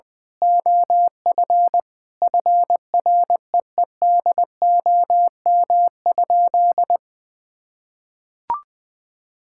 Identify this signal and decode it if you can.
Morse code, heard as a keyed tone, about 20 wpm, 700 Hz. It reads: OF FREEDOM?
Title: Sentences from Top 1600 Words 20wpm